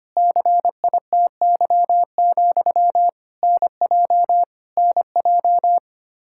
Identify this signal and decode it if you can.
Morse code, heard as a keyed tone, about 25 wpm, 700 Hz. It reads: CITY, NJ NJ
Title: Practice QSO - Exchange 1 - Beginning - 25wpm